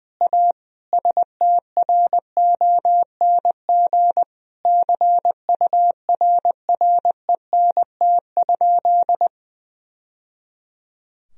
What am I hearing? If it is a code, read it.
Morse code, heard as a keyed tone, about 21 wpm, 700 Hz. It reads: A STRONG CURRENT?
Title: Sentences from Top 1000 Words - Review Part 4 - 20wpm